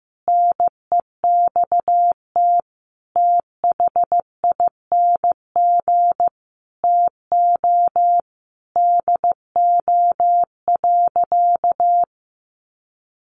Morse code, heard as a keyed tone, about 15 wpm, 700 Hz. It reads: NEXT THING TO DO.
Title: Sentences from Top 1300 Words 15wpm